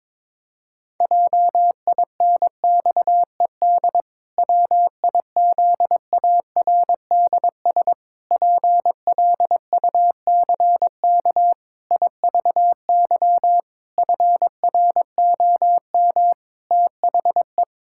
Morse code, heard as a keyed tone, about 22 wpm, 700 Hz. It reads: JINXED WIZARDS PLUCK IVY FROM THE
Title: Pangram Challenge v2 - 22wpm